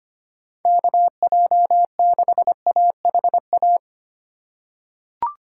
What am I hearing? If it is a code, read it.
Morse code, heard as a keyed tone, about 25 wpm, 700 Hz. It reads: KJ6AHA
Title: USA Call Signs - 6 Characters - 4x Character Spacing - 25wpm